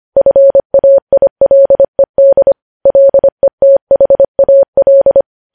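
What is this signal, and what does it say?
Morse code, heard as a keyed tone, about 25 wpm, 550 Hz. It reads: FAILED LETHAL
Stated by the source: News headlines sent in Morse at 25 wpm.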